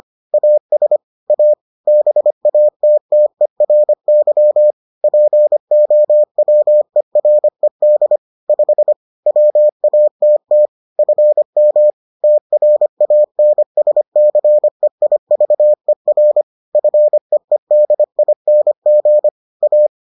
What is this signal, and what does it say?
Morse code, heard as a keyed tone, about 25 wpm, 600 Hz. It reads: AS A BATTERY POWERED 5 WATT FM TRANSCEIVER FEEDING A